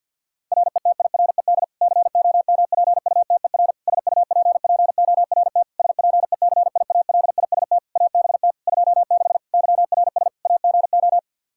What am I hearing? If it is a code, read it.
Morse code, heard as a keyed tone, about 50 wpm, 700 Hz. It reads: RETIRED COMPUTER SUPPORT SPECIALIST ABT 26 YRS AGO
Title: Practice QSO - Exchange 3 - More Info - 50wpm